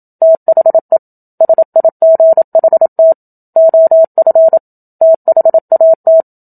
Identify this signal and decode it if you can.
Morse code, heard as a keyed tone, about 27 wpm, 650 Hz. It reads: THE SIGHT OF THAT
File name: Book Two / 01-Chapter 1 - 27 WPM